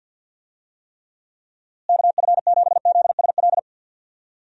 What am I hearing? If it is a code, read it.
Morse code, heard as a keyed tone, about 51 wpm, 700 Hz. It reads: KF7ZSL
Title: USA Call Signs - 6 Characters - Repeated 2x - 50wpm